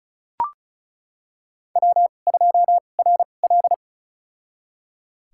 Morse code, heard as a keyed tone, about 34 wpm, spaced as for 25 wpm, 700 Hz. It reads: W2RL
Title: USA Call Signs - 4 Characters - 2x Character Spacing - 35wpm